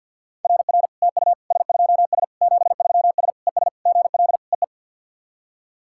Morse code, heard as a keyed tone, about 48 wpm, 700 Hz. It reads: RR TU S2S 73S ES GL EE
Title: SOTA QSO Practice 50wpm